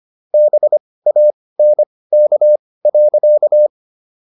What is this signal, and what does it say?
Morse code, heard as a keyed tone, about 25 wpm, spaced as for 18 wpm, 600 Hz. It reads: BANK.